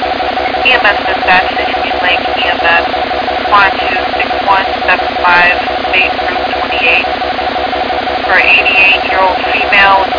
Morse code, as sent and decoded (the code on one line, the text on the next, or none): none